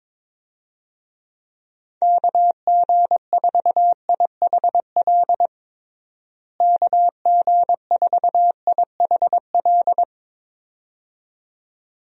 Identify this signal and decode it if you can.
Morse code, heard as a keyed tone, about 22 wpm, 700 Hz. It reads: KG4IHL KG4IHL
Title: USA Call Signs - 6 Characters - Repeated 2x - 22wpm